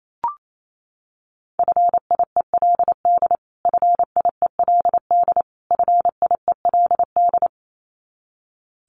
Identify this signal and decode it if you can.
Morse code, heard as a keyed tone, about 28 wpm, 700 Hz. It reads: FIELD FIELD FIELD
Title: Top 500 Words Repeated 28wpm